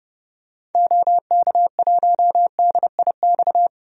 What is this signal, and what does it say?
Morse code, heard as a keyed tone, about 32 wpm, 700 Hz. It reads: OK1DIX
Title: DXCC-100 Call Signs - 6 Characters - 30wpm